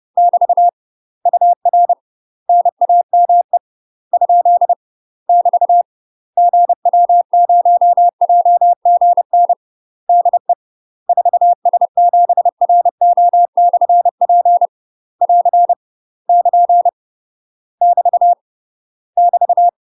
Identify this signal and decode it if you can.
Morse code, heard as a keyed tone, about 30 wpm, 700 Hz. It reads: <BT> UR NAME ? <BT> GW0JGN DE 4S7RO/P <AR> <KN> <BT> <BT>
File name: UK MASTER QSOs TRAINING / 30WPM / UK MASTER QSOs 0023